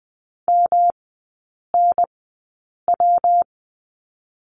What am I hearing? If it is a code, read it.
Morse code, heard as a keyed tone, about 20 wpm, 700 Hz. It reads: M N W
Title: 198 Lesson 32e - Letter k - Three Characters 20wpm